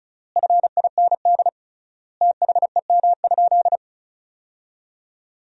Morse code, heard as a keyed tone, about 36 wpm, 700 Hz. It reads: FIND THEM?
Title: 3x Word-Spacing - Sentences from Top 200 Words - 35wpm